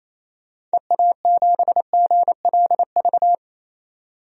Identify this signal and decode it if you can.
Morse code, heard as a keyed tone, about 28 wpm, 700 Hz. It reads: EA7GLV
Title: DXCC-100 Call Signs - 6 Characters - 28wpm